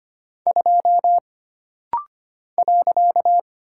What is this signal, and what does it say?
Morse code, heard as a keyed tone, about 25 wpm, 700 Hz. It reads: 2 .
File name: Single Character (Letter:Number:Punctuation:Prosign) - Rapid-Fire - 25wpm